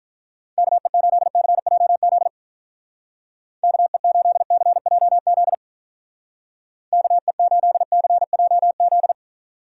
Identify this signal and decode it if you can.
Morse code, heard as a keyed tone, about 45 wpm, 700 Hz. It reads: KE8CJZ KE8CJZ KE8CJZ